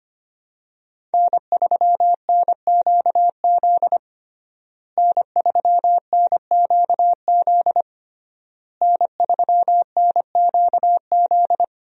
Morse code, heard as a keyed tone, about 25 wpm, 700 Hz. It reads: N3NQZ N3NQZ N3NQZ
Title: USA Call Signs - 5 Characters - Repeated 3x - 25wpm